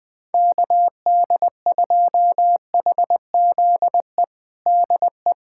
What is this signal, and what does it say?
Morse code, heard as a keyed tone, about 21 wpm, 700 Hz. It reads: KD2HZE DE
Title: USA Callsign DE Callsign - 6 Characters - 20wpm